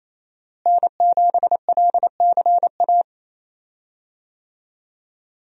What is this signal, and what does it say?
Morse code, heard as a keyed tone, about 27 wpm, 700 Hz. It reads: N7LCA
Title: USA Call Signs - 5 Characters - Repeated 2x - 28wpm